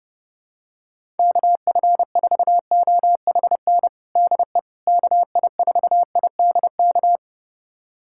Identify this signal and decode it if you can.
Morse code, heard as a keyed tone, about 30 wpm, 700 Hz. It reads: KF4OHN DE KI4IDK
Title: USA Callsign DE Callsign - 6 Characters - 30wpm